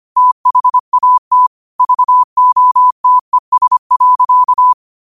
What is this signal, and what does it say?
Morse code, heard as a keyed tone, about 25 wpm, 1000 Hz. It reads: THAT VOTES.
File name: qotdfiles / 25wpm